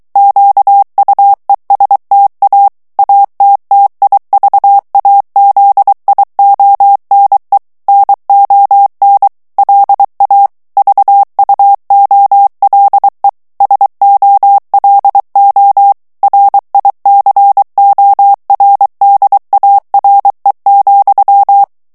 Morse code, heard as a keyed tone, about 23 wpm, 800 Hz. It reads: QUESTA ATTIVAZIONE NON LA VUOLE SOLO RICORDARE,
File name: morse mp3 / ITALIEN texte clair 1 vitesse 22 mots minute